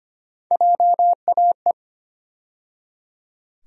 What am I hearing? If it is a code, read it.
Morse code, heard as a keyed tone, about 25 wpm, 700 Hz. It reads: JAE
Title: USA Call Sign Postfixes - 3 Letters 25wpm